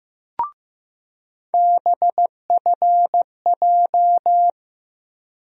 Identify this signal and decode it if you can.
Morse code, heard as a keyed tone, about 15 wpm, 700 Hz. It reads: BFJ
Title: USA Call Sign Postfixes - 3 Letters 15wpm